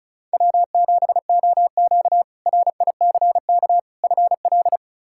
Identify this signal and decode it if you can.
Morse code, heard as a keyed tone, about 36 wpm, 700 Hz. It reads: W7OQ RICK FL
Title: K1USN SST QSO Practice - 35wpm